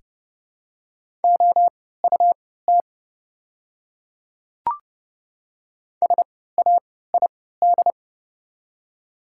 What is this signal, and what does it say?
Morse code, heard as a keyed tone, about 30 wpm, spaced as for 17 wpm, 700 Hz. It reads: OUT SAID
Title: 3x Character Spacing - Top 100 Words - 30wpm